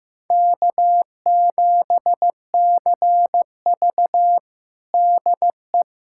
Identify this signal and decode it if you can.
Morse code, heard as a keyed tone, about 15 wpm, 700 Hz. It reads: K7CV DE
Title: USA Callsign DE Callsign - 4 Characters - 15wpm